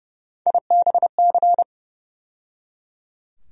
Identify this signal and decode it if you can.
Morse code, heard as a keyed tone, about 30 wpm, 700 Hz. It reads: IBC
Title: USA Call Sign Postfixes - 3 Letters 30wpm